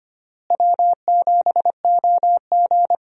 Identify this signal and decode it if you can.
Morse code, heard as a keyed tone, about 25 wpm, 700 Hz. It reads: W7OG
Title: USA Call Signs - 4 Characters - Repeated 3x - 25wpm